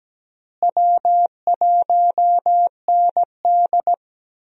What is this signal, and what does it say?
Morse code, heard as a keyed tone, about 17 wpm, 700 Hz. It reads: W1ND
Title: USA Call Signs - 4 Characters - Repeated 2x - 17wpm